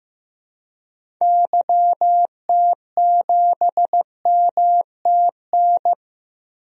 Morse code, heard as a keyed tone, about 15 wpm, 700 Hz. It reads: YT7MTN
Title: DXCC-100 Call Signs - 6 Characters - 15wpm